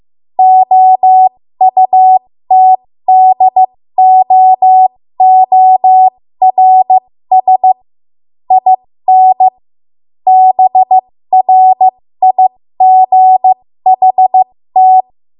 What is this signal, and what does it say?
Morse code, heard as a keyed tone, about 15 wpm, spaced as for 13 wpm, 750 Hz. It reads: OUTDOORS IN BRIGHT
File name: Morse / 13WPM